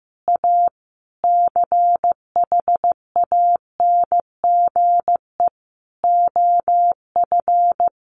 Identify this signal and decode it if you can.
Morse code, heard as a keyed tone, about 14 wpm, 700 Hz. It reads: A CHANGE OF
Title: Sentences from Top 1200 Words 15wpm